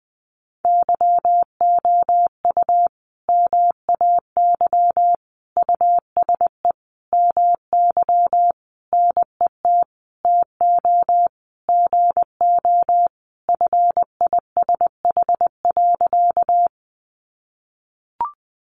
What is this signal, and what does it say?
Morse code, heard as a keyed tone, about 20 wpm, 700 Hz. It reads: YOU MAY USE MY NET TO GO FISH.
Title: Sentences from Top 1400 Words 20wpm